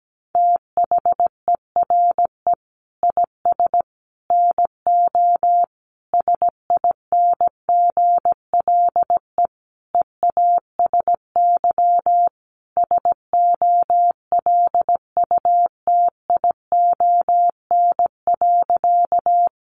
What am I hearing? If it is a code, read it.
Morse code, heard as a keyed tone, about 17 wpm, 700 Hz. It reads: THERE IS NO SINGLE EASY SOLUTION.
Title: Sentences from Top 1300 Words 17wpm